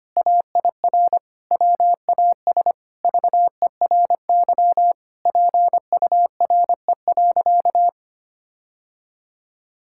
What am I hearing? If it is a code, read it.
Morse code, heard as a keyed tone, about 25 wpm, 700 Hz. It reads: AIR WAS VERY PURE.
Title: Sentences from Top 1600 Words 25wpm